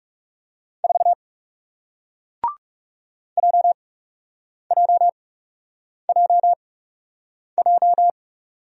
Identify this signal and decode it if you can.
Morse code, heard as a keyed tone, about 40 wpm, spaced as for 8 wpm, 700 Hz. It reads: 4 JJJJ